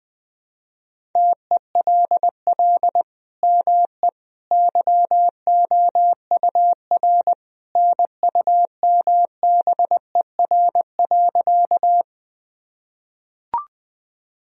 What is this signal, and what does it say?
Morse code, heard as a keyed tone, about 20 wpm, 700 Hz. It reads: TELL ME YOUR NUMBER.
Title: Sentences from Top 1000 Words - Review Part 1 - 20wpm